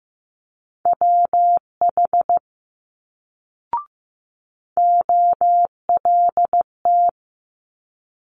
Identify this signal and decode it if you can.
Morse code, heard as a keyed tone, about 15 wpm, 700 Hz. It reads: WH OLT
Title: Multi-letter Phonemes Quick Fire 15wpm